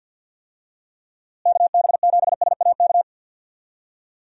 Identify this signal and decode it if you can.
Morse code, heard as a keyed tone, about 49 wpm, 700 Hz. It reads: KB7SAK
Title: USA Call Signs - 6 Characters - Repeated 2x - 50wpm